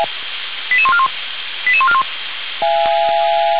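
Five-tone selective call (ZVEI), heard as quick runs of five tones, tones of about 70 ms. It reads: 88141 88141
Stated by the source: ZVEI five-tone selective call received on SDR, from the Feuerwehr Orlamünde audio server.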